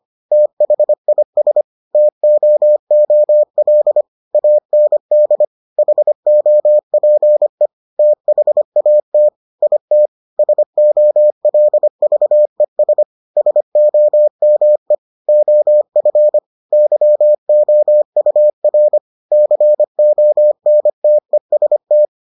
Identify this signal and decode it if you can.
Morse code, heard as a keyed tone, about 25 wpm, 600 Hz. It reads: THIS TOOL AND HOPE THAT IT SOLVES SOME OF YOUR CONTEST